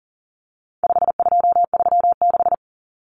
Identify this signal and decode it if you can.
Morse code, heard as a keyed tone, about 40 wpm, 700 Hz. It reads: 5236
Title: Numbers - 4-Digit 40wpm